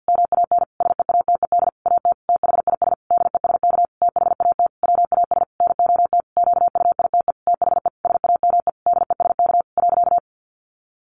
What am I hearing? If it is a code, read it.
Morse code, heard as a keyed tone, about 50 wpm, 700 Hz. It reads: MAN SEATED AT THIS DESK THAT WAS NOT QUITE THE SAME DESK .
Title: A Sound Of Thunder 50wpm